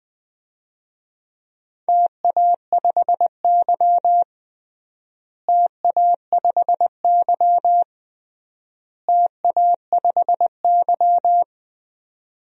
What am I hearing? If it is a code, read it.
Morse code, heard as a keyed tone, about 20 wpm, 700 Hz. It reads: TA5Y TA5Y TA5Y
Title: DXCC-100 Call Signs - 4 Characters - Repeated 3x - 20wpm